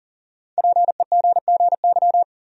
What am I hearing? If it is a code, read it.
Morse code, heard as a keyed tone, about 39 wpm, 700 Hz. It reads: PEGGY
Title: USA Common Names - Max 5 Letters - 40wpm